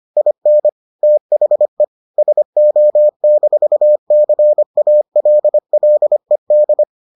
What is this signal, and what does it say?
Morse code, heard as a keyed tone, about 25 wpm, 600 Hz. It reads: IN THE SO-CALLED